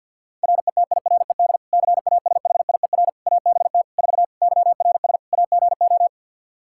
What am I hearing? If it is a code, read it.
Morse code, heard as a keyed tone, about 52 wpm, 700 Hz. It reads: RETIRED CASHIER ABT 4 YRS AGO
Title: Practice QSO - Exchange 3 - More Info - 50wpm